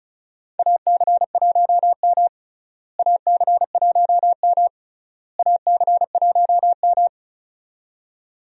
Morse code, heard as a keyed tone, about 36 wpm, 700 Hz. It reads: AC1M AC1M AC1M
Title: USA Call Signs - 4 Characters - Repeated 3x - 35wpm